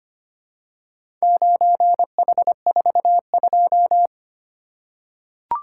Morse code, heard as a keyed tone, about 25 wpm, 700 Hz. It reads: 9H42
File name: practice / 25wpm / Four Letter-Number Combinations - 25wpm